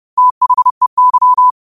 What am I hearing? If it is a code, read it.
Morse code, heard as a keyed tone, about 30 wpm, 1000 Hz. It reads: THEY
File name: qotdfiles / 30wpm